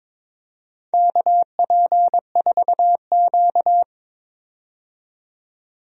Morse code, heard as a keyed tone, about 22 wpm, 700 Hz. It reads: KP4Q
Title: DXCC-100 Call Signs - 4 Characters - 22wpm